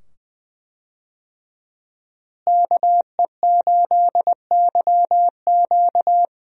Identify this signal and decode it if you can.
Morse code, heard as a keyed tone, about 20 wpm, 700 Hz. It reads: KE8YQ
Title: USA Call Signs - 5 Characters - Repeated 2x - 20wpm